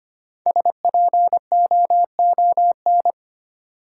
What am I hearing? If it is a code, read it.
Morse code, heard as a keyed tone, about 25 wpm, 700 Hz. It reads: SPOON
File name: practice / 25wpm / Easy Words with Hints - 25wpm